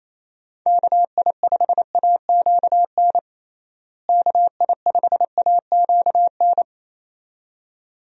Morse code, heard as a keyed tone, about 28 wpm, 700 Hz. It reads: KI5AQN KI5AQN
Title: USA Call Signs - 6 Characters - Repeated 2x - 28wpm